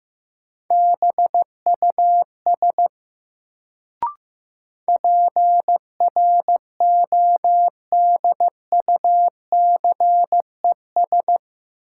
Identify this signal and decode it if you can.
Morse code, heard as a keyed tone, about 15 wpm, 700 Hz. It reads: BUS PRODUCES
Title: Sentences from Top 1000 Words - Review Part 6 - New Word Drill - 15wpm